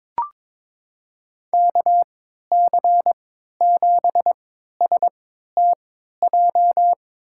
Morse code, heard as a keyed tone, about 22 wpm, spaced as for 13 wpm, 700 Hz. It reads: KC7STJ
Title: USA Call Signs - 6 Characters - 3x Character Spacing - 22wpm